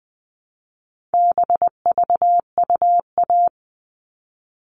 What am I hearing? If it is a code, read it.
Morse code, heard as a keyed tone, about 20 wpm, 700 Hz. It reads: BVUA
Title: All Cut Numbers - 4-Digit 20wpm